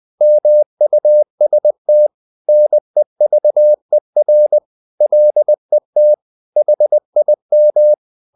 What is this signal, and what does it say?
Morse code, heard as a keyed tone, about 20 wpm, 600 Hz. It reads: MUST NEVER LET HIM
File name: the old man and the sea / the old man and the sea part 1 20wpm